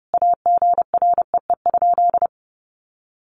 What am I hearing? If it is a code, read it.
Morse code, heard as a keyed tone, about 30 wpm, 700 Hz. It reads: AGREE?
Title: Sentences from Top 1100 Words 30wpm